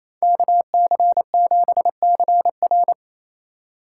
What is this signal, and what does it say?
Morse code, heard as a keyed tone, about 33 wpm, 700 Hz. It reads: KC7CR
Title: Speed Racing - USA 2x2 Call Signs 28wpm